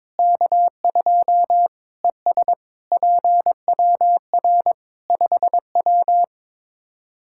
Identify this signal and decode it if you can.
Morse code, heard as a keyed tone, about 21 wpm, 700 Hz. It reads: K2 ES PWR 5W
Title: Practice QSO - Exchange 2 - Info - 22wpm